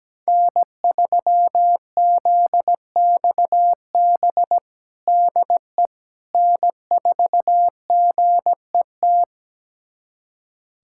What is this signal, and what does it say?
Morse code, heard as a keyed tone, about 17 wpm, 700 Hz. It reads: N3ZXB DE N4GET
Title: USA Callsign DE Callsign - 5 Characters - 17wpm